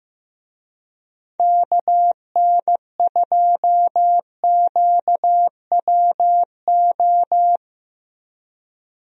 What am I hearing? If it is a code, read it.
Morse code, heard as a keyed tone, about 15 wpm, 700 Hz. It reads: KN2QWO
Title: USA Call Signs - 6 Characters - Repeated 2x - 15wpm